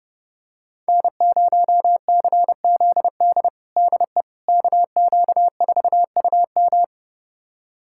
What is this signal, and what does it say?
Morse code, heard as a keyed tone, about 30 wpm, 700 Hz. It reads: N0CZD DE KQ4UM
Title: USA Callsign DE Callsign - 5 Characters - 30wpm